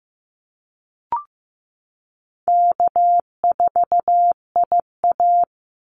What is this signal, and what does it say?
Morse code, heard as a keyed tone, about 15 wpm, 700 Hz. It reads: K4IA
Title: Easy Call Signs Repeated 3x - 15wpm